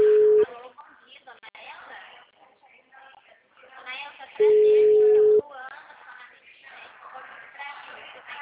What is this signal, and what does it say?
Ringback tone (Europe, 425 Hz), with speech.
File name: out- -2048- - - .809